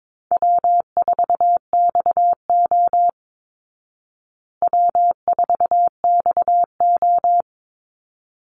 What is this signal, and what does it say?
Morse code, heard as a keyed tone, about 22 wpm, 690 Hz. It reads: W4XO W4XO
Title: Easy Call Signs Repeated 2x - 22wpm